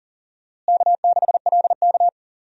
Morse code, heard as a keyed tone, about 40 wpm, 700 Hz. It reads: K6LK
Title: USA Call Signs - 4 Characters - Repeated 3x - 40wpm